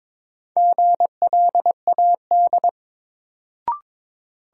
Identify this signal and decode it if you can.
Morse code, heard as a keyed tone, about 22 wpm, 700 Hz. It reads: GLAD
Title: Sentences from Top 800 Words - New Word Drill - 22wpm